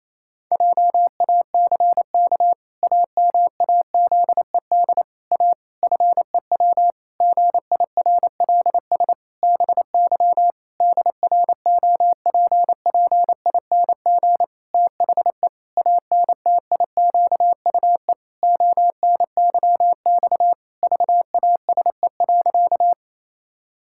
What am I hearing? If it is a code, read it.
Morse code, heard as a keyed tone, about 28 wpm, 700 Hz. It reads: JACK AMAZED A FEW GIRLS BY DROPPING THE ANTIQUE ONYX VASE.
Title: Pangram Challenge v2 - 28wpm